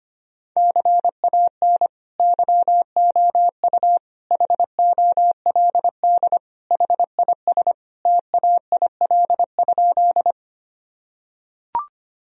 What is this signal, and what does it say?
Morse code, heard as a keyed tone, about 25 wpm, 700 Hz. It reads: CAN YOU HOLD HIS TAIL?
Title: Sentences from Top 1000 Words - Review Part 3 - 25wpm